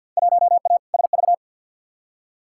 Morse code, heard as a keyed tone, about 47 wpm, 700 Hz. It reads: 1A SV
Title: Field Day Exchanges Encore - 50wpm